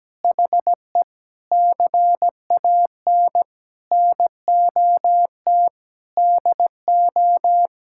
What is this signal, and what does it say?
Morse code, heard as a keyed tone, about 17 wpm, 700 Hz. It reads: HE CAN NOT DO
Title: Sentences from Top 1000 Words - Review Part 6 - 17wpm